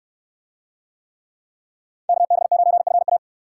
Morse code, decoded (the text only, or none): KB9FA